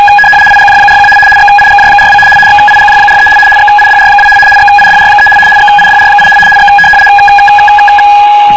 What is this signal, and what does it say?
RTTY, about 75 baud. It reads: IGJ41 XIGJ42 XIGJ43 XIDR2 XIDR3 XIDR8 X